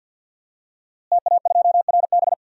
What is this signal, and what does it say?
Morse code, heard as a keyed tone, about 51 wpm, 700 Hz. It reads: TA2RB
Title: DXCC-100 Call Signs - 5 Characters - Repeated 2x - 50wpm